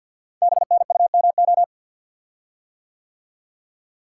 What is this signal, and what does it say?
Morse code, heard as a keyed tone, about 50 wpm, 690 Hz. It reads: BNUMO
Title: Code Groups - 5 Letters - 50wpm